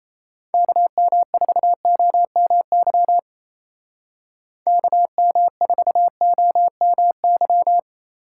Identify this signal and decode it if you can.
Morse code, heard as a keyed tone, about 33 wpm, 700 Hz. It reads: KM4OMY KM4OMY
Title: Speed Racing - USA 2x3 Call Signs 28wpm